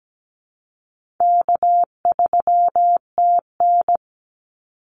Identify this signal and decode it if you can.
Morse code, heard as a keyed tone, about 17 wpm, 700 Hz. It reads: K3TN
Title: Easy Call Signs - 17wpm